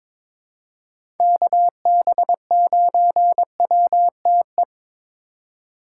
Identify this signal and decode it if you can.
Morse code, heard as a keyed tone, about 21 wpm, 700 Hz. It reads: KB9WTE
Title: USA Call Signs - 6 Characters - 4x Character Spacing - 22wpm